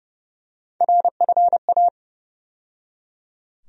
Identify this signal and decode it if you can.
Morse code, heard as a keyed tone, about 30 wpm, 700 Hz. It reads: RFA
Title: USA Call Sign Postfixes - 3 Letters 30wpm